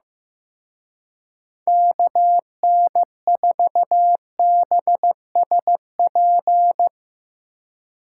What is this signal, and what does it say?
Morse code, heard as a keyed tone, about 16 wpm, 700 Hz. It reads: KN4BSP
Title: Speed Racing - USA 2x3 Call Signs 15wpm